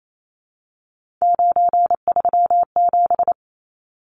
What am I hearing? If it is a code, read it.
Morse code, heard as a keyed tone, about 28 wpm, 700 Hz. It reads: 937
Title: Numbers - 3-Digit 28wpm